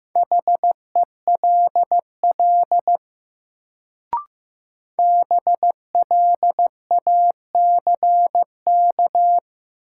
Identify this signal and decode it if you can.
Morse code, heard as a keyed tone, about 15 wpm, 700 Hz. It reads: HELL BLACK
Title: Sets of 5 Words Encore - New Word Drill - 15wpm